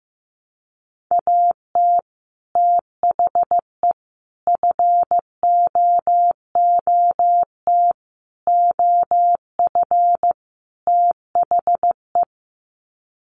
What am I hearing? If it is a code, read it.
Morse code, heard as a keyed tone, about 14 wpm, 700 Hz. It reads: AT THE FOOT OF THE
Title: Sets of 5 Words 15wpm